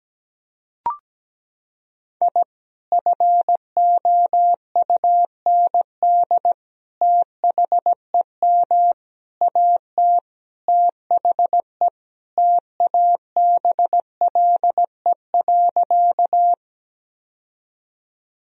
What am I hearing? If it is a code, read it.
Morse code, heard as a keyed tone, about 17 wpm, 700 Hz. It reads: I FOUND THEM AT THE TABLE.
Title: Sentences from Top 1000 Words - Review Part 2 - 17wpm